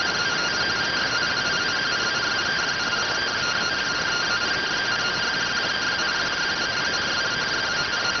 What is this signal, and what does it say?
Two-tone teleprinter signal, about 170 Hz shift.